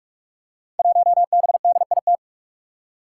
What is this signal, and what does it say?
Morse code, heard as a keyed tone, about 44 wpm, 700 Hz. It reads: 1BDIT
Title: Five Challenging Letter-Number Combinations 45wpm